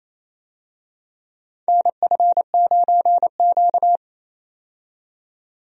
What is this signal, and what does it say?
Morse code, heard as a keyed tone, about 28 wpm, 700 Hz. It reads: NF9Q
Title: USA Call Signs - 4 Characters - 28wpm